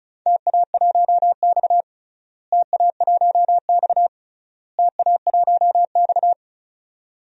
Morse code, heard as a keyed tone, about 36 wpm, 700 Hz. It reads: TA1X TA1X TA1X
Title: DXCC-100 Call Signs - 4 Characters - Repeated 3x - 35wpm